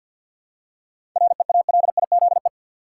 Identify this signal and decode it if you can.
Morse code, heard as a keyed tone, about 51 wpm, 700 Hz. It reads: REALIZE
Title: Seven Letter Words - 50wpm